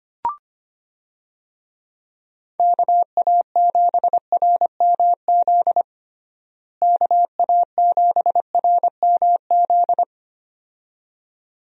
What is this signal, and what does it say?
Morse code, heard as a keyed tone, about 25 wpm, 700 Hz. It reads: KA7RMZ KA7RMZ
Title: USA Call Signs - 6 Characters - Repeated 2x - 25wpm